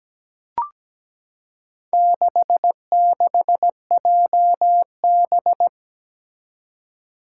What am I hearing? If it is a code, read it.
Morse code, heard as a keyed tone, about 17 wpm, 700 Hz. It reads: 66JB
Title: Four Challenging Letter-Number Combinations 17wpm